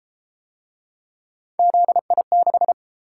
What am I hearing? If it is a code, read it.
Morse code, heard as a keyed tone, about 30 wpm, 700 Hz. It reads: ZI6